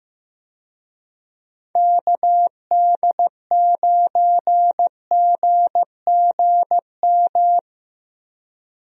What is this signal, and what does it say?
Morse code, heard as a keyed tone, about 15 wpm, 700 Hz. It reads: KD9GGM
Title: USA Call Signs - 6 Characters - Repeated 3x - 15wpm